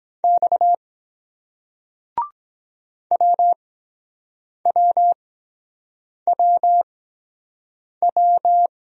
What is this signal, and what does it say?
Morse code, heard as a keyed tone, about 23 wpm, spaced as for 7 wpm, 700 Hz. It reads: X WWWW